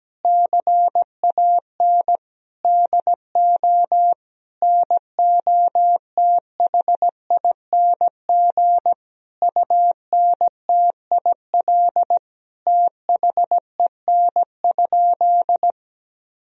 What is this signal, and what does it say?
Morse code, heard as a keyed tone, about 17 wpm, 700 Hz. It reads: CAN DO NOTHING UNTIL THEN?
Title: Sentences from Top 1000 Words - Review Part 1 - 17wpm